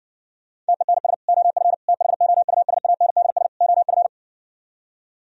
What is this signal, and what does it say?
Morse code, heard as a keyed tone, about 60 wpm, 700 Hz. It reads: TENS OF THOUSANDS OF